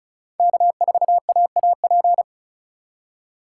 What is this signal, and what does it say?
Morse code, heard as a keyed tone, about 34 wpm, 690 Hz. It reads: K4AAP
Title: USA Call Signs - 5 Characters - 2x Character Spacing - 35wpm